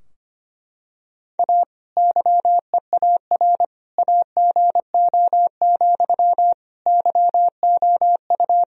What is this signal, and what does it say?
Morse code, heard as a keyed tone, about 25 wpm, 700 Hz. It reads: A YEAR AGO, YOU
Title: Sentences from Top 1000 Words - Review Part 3 - 25wpm